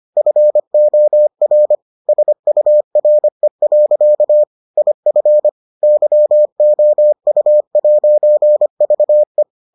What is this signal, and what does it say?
Morse code, heard as a keyed tone, about 25 wpm, 600 Hz. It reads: FOR SURE. IF YOU'VE